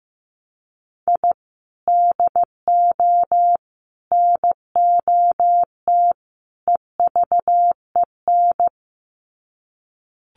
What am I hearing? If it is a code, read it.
Morse code, heard as a keyed tone, about 15 wpm, 700 Hz. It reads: I DO NOT EVEN
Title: Sets of 4 Words 15wpm